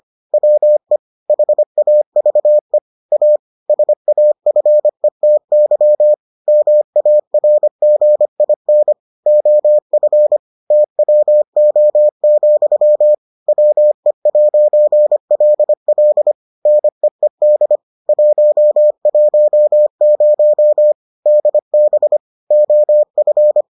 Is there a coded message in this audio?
Yes: Morse code from 0.3 to 23.6 s, heard as a keyed tone, about 25 wpm, 600 Hz. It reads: WE HAVE A SAFETY MARGIN OF TWO, WE'LL NEED 110 DB OF